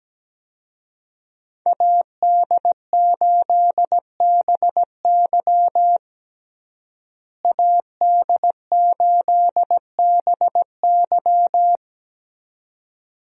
Morse code, heard as a keyed tone, about 17 wpm, 700 Hz. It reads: AD8BY AD8BY
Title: USA Call Signs - 5 Characters - Repeated 2x - 17wpm